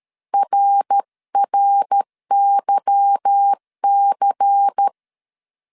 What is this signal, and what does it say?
Morse code, heard as a keyed tone, about 13 wpm, 800 Hz. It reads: RRYC
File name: CW / DET / 1 9 1500C S A 12wpm 3800